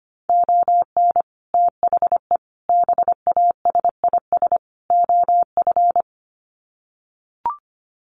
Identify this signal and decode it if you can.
Morse code, heard as a keyed tone, about 25 wpm, 700 Hz. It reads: ON THE BASIS OF
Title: Sets of 4 Words 25wpm